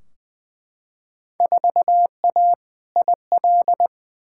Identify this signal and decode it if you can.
Morse code, heard as a keyed tone, about 20 wpm, 700 Hz. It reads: 4A IL
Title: Field Day Exchanges - 20wpm